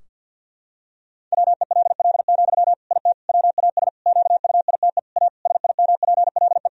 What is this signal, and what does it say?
Morse code, heard as a keyed tone, about 50 wpm, 700 Hz. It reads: WELL, IT WAS QUITE A SIMPLE
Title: Sentences from Top 1000 Words - Review Part 3 - 50wpm